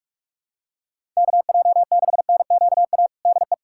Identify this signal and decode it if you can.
Morse code, heard as a keyed tone, about 42 wpm, 700 Hz. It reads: KJ6NQA DE
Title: USA Callsign DE Callsign 45wpm